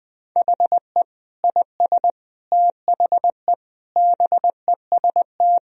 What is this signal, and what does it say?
Morse code, heard as a keyed tone, about 21 wpm, 700 Hz. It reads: HE IS THE BEST
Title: Sentences from Top 1000 Words - Review Part 4 - 20wpm